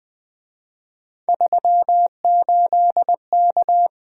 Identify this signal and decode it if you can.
Morse code, heard as a keyed tone, about 24 wpm, 700 Hz. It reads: 38K
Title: Speed-Racing - Three Letter-Number Combinations - 20wpm